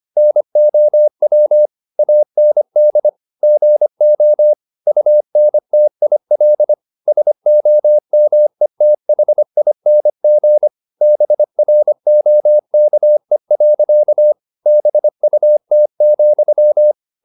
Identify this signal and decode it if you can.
Morse code, heard as a keyed tone, about 25 wpm, 600 Hz. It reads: NOW AND GO UNTIL SOMETHING BROKE. BUT,
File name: the old man and the sea part 1 25wpm